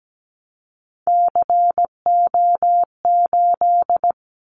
Morse code, heard as a keyed tone, about 17 wpm, 700 Hz. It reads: CO8
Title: Callsign Prefixes for Top 100 DXCC - 17wpm